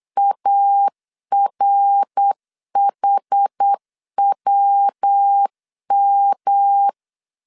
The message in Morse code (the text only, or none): ARHWM